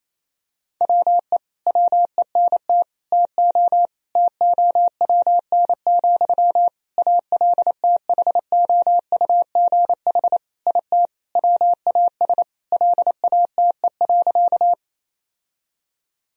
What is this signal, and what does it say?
Morse code, heard as a keyed tone, about 28 wpm, 700 Hz. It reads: WE WENT TO TOWN, ALTHOUGH IT WAS LATE.
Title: Sentences from Top 1000 Words - Review Part 3 - 28wpm